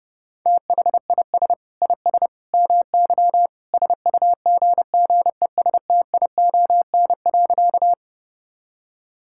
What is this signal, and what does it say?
Morse code, heard as a keyed tone, about 32 wpm, 700 Hz. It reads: THIS IS MY SUGGESTION.
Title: Sentences from Top 1600 Words 30wpm